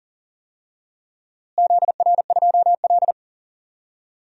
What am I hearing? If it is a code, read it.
Morse code, heard as a keyed tone, about 38 wpm, 700 Hz. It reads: ZR2L
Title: DXCC-100 Call Signs - 4 Characters - Repeated 2x - 40wpm